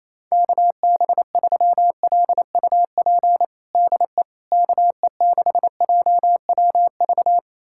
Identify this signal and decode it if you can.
Morse code, heard as a keyed tone, about 28 wpm, 700 Hz. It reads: KB3LUP DE KE6JWV
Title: USA Callsign DE Callsign 28wpm